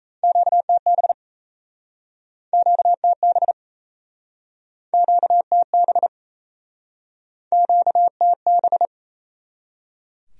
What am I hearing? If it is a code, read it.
Morse code, heard as a keyed tone, about 39 wpm, 700 Hz. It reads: QTB QTB QTB QTB